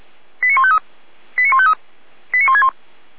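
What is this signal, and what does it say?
Five-tone selective call (ZVEI), heard as quick runs of five tones, tones of about 70 ms. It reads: 89153 89153 89161
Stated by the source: Five-tone ZVEI selective call received by SDR, archived by Feuerwehr Orlamünde.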